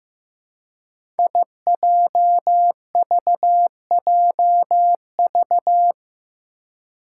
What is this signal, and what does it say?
Morse code, heard as a keyed tone, about 15 wpm, 700 Hz. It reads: IJVJV
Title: Five Challenging Letter-Number Combinations 15wpm